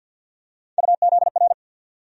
Morse code, heard as a keyed tone, about 45 wpm, 700 Hz. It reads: UZR
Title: USA Call Sign Postfixes - 3 Letters 50wpm